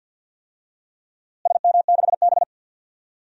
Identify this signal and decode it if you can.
Morse code, heard as a keyed tone, about 53 wpm, 700 Hz. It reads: SM6B
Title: DXCC-100 Call Signs - 4 Characters - Repeated 3x - 50wpm